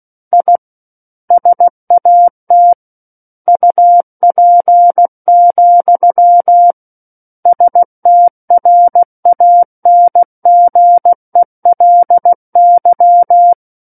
Morse code, heard as a keyed tone, about 16 wpm, 700 Hz. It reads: I SAT UP, STRANGELY
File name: Book One / 07-Chapter 7 - 16 WPM